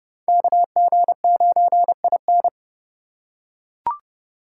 Morse code, heard as a keyed tone, about 26 wpm, 700 Hz. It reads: KG9IN